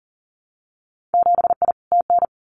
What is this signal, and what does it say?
Morse code, heard as a keyed tone, about 40 wpm, 700 Hz. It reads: 7I TN
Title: Winter Field Day Exchanges Encore 40wpm